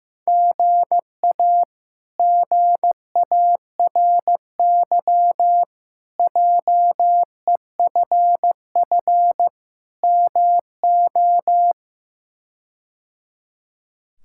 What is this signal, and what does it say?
Morse code, heard as a keyed tone, about 16 wpm, 700 Hz. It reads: GA GARY JEFF MO
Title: K1USN SST QSO Practice - 15wpm